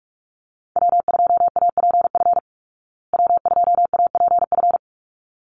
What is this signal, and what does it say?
Morse code, heard as a keyed tone, about 44 wpm, 700 Hz. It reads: W2APF W2APF
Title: Call Signs Repeated 3x - 45wpm